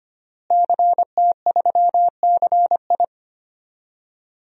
Morse code, heard as a keyed tone, about 25 wpm, 700 Hz. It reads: CT3CI
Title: DXCC-100 Call Signs - 5 Characters - Repeated 3x - 25wpm